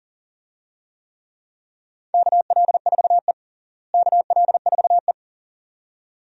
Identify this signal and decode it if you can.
Morse code, heard as a keyed tone, about 40 wpm, 700 Hz. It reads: KL4E KL4E
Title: USA Call Signs - 4 Characters - Repeated 2x - 40wpm